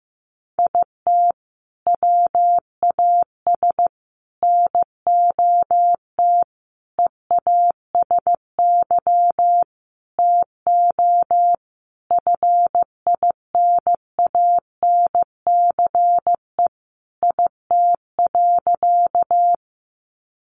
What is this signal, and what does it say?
Morse code, heard as a keyed tone, about 15 wpm, 700 Hz. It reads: IT WAS NOT EASY TO FINANCE IT.
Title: Sentences from Top 1500 Words 15wpm